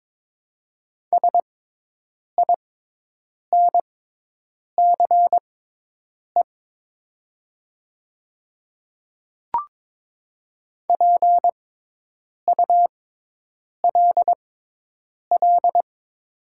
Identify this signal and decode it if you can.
Morse code, heard as a keyed tone, about 22 wpm, spaced as for 8 wpm, 700 Hz. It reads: SINCE PULL
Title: 6x Character Spacing - Top 500 Words - 22wpm